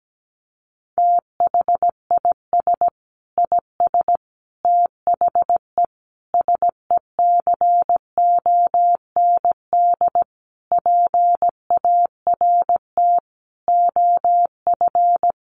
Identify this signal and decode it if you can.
Morse code, heard as a keyed tone, about 17 wpm, 700 Hz. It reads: THIS IS THE SECOND PART OF
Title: Sentences from Top 1400 Words 17wpm